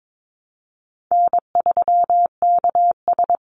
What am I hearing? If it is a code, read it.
Morse code, heard as a keyed tone, about 22 wpm, 700 Hz. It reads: N3KS
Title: Call Signs Repeated 3x - 22wpm